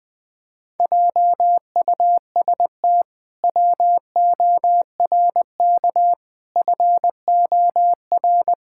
Morse code, heard as a keyed tone, about 20 wpm, 700 Hz. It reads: JUST WORK FOR
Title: Pangram Challenge v2 - 20wpm